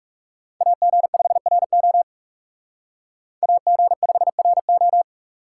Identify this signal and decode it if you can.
Morse code, heard as a keyed tone, about 36 wpm, 700 Hz. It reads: AG5RO AG5RO
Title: Speed Racing - USA Call Signs 30wpm